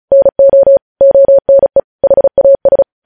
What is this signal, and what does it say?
Morse code, heard as a keyed tone, about 36 wpm, 550 Hz. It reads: NO ONE HAS
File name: podcast / News headlines-1x-35x35